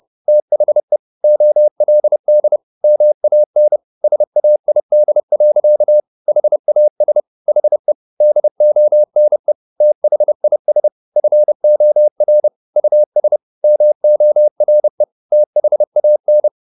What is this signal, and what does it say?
Morse code, heard as a keyed tone, about 30 wpm, 600 Hz. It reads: THE OLD MAN SAID. HAS HE DONE THIS FOR US MORE THAN
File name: the old man and the sea / the old man and the sea 1 30wpm